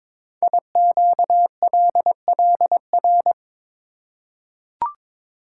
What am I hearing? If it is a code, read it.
Morse code, heard as a keyed tone, about 22 wpm, 700 Hz. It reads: IQLLR
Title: Code Groups - 5 Letters - 22wpm